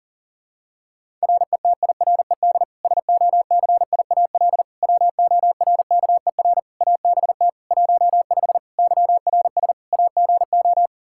Morse code, heard as a keyed tone, about 40 wpm, 700 Hz. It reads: RETIRED SOCIAL WORKER ABT 15 YRS AGO
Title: Practice QSO - Exchange 3 - More Info - 40wpm